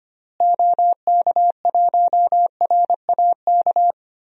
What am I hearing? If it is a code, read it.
Morse code, heard as a keyed tone, about 25 wpm, 700 Hz. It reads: OK1RAK
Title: DXCC-100 Call Signs - 6 Characters - Repeated 3x - 25wpm